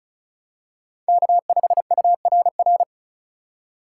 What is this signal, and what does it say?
Morse code, heard as a keyed tone, about 36 wpm, 700 Hz. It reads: K5URR
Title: USA Call Signs - 5 Characters - 2x Character Spacing - 35wpm